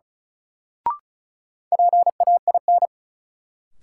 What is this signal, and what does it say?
Morse code, heard as a keyed tone, about 36 wpm, 700 Hz. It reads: PAIN
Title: Sentences from Top 1600 Words - New Word Drill Alt - 35wpm